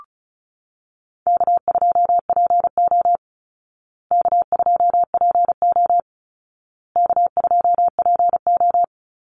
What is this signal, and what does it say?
Morse code, heard as a keyed tone, about 34 wpm, 700 Hz. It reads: K2PO K2PO K2PO
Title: Easy Call Signs Repeated 3x - 35wpm